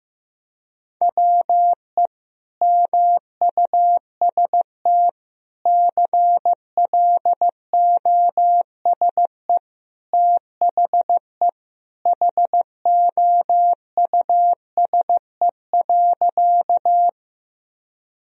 Morse code, heard as a keyed tone, about 16 wpm, 700 Hz. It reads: WE MUST CLOSE THE HOUSE.
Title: Sentences from Top 1000 Words - Review Part 1 - 15wpm